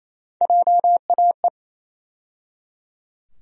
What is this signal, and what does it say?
Morse code, heard as a keyed tone, about 28 wpm, 700 Hz. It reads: JAE
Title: USA Call Sign Postfixes - 3 Letters 28wpm